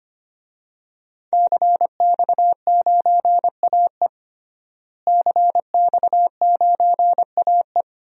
Morse code, heard as a keyed tone, about 25 wpm, 700 Hz. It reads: CX9AE CX9AE
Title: DXCC-100 Call Signs - 5 Characters - Repeated 2x - 25wpm